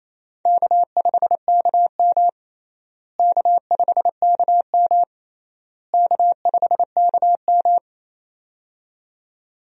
Morse code, heard as a keyed tone, about 28 wpm, 700 Hz. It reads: K5KM K5KM K5KM
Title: USA Call Signs - 4 Characters - Repeated 3x - 28wpm